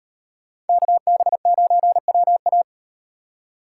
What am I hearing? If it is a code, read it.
Morse code, heard as a keyed tone, about 38 wpm, 700 Hz. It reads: KB9WA